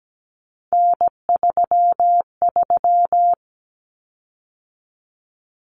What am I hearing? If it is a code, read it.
Morse code, heard as a keyed tone, about 17 wpm, 700 Hz. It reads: N33
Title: Common Cut Numbers - 3-Digit 17wpm